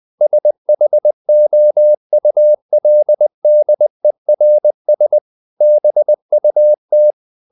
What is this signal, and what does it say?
Morse code, heard as a keyed tone, about 20 wpm, 600 Hz. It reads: SHOULDERS BUT
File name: the old man and the sea part 2 20wpm